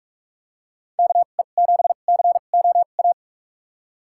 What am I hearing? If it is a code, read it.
Morse code, heard as a keyed tone, about 46 wpm, spaced as for 32 wpm, 700 Hz. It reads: KE7COA
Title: USA Call Signs - 6 Characters - 2x Character Spacing - 45wpm